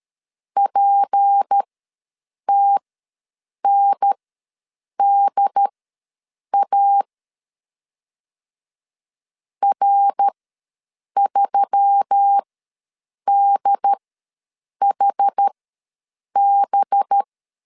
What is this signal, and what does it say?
Morse code, heard as a keyed tone, about 13 wpm, spaced as for 7 wpm, 800 Hz. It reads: PTNDA R3DHB
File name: LEZIONI / 6 PKX0 / 6 P ws 12wpm 3117